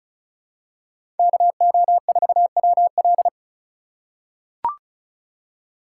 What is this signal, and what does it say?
Morse code, heard as a keyed tone, about 36 wpm, 700 Hz. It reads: KO4WL
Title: USA Call Signs - 5 Characters - 3x Character Spacing - 35wpm